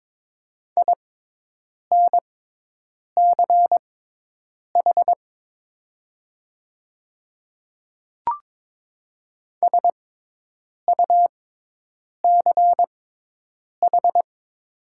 Morse code, heard as a keyed tone, about 22 wpm, spaced as for 8 wpm, 700 Hz. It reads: INCH SUCH
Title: 6x Character Spacing - Top 500 Words - 22wpm